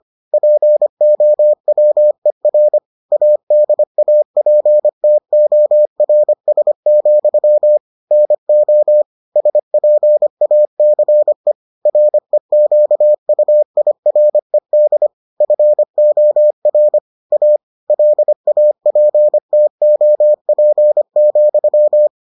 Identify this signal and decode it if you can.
Morse code, heard as a keyed tone, about 25 wpm, 600 Hz. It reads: POWER ADAPTORS, NO SPACE REQUIRED FOR A LAPTOP,